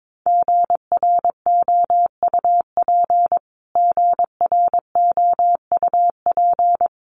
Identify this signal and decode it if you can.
Morse code, heard as a keyed tone, about 22 wpm, 700 Hz. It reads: GROUP GROUP
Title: Top 500 Words Repeated 22wpm